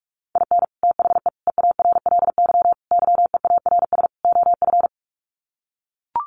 Morse code, heard as a keyed tone, about 44 wpm, 700 Hz. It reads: IN THE EARLY YEARS OF
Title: Sets of 5 Words Encore 45wpm